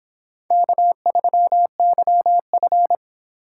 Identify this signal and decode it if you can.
Morse code, heard as a keyed tone, about 23 wpm, 700 Hz. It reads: K3YF